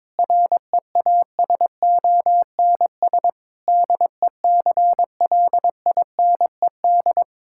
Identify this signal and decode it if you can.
Morse code, heard as a keyed tone, about 22 wpm, 700 Hz. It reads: REASONS DECLINED
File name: practice / 22wpm / Sentences with 8-Word Phrases - 22wpm